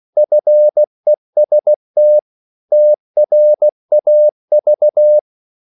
Morse code, heard as a keyed tone, about 16 wpm, 600 Hz. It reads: FEST TRAV
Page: Korte danske ord 16wpm